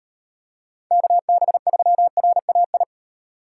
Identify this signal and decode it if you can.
Morse code, heard as a keyed tone, about 38 wpm, 700 Hz. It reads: KB3RAI